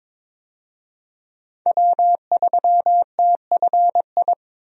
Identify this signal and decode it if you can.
Morse code, heard as a keyed tone, about 22 wpm, 700 Hz. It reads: W3TFI
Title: USA Call Signs - 5 Characters - Repeated 3x - 22wpm